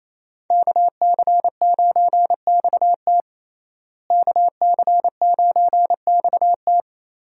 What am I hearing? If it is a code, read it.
Morse code, heard as a keyed tone, about 27 wpm, 700 Hz. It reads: KC9XT KC9XT
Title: USA Call Signs - 5 Characters - Repeated 2x - 28wpm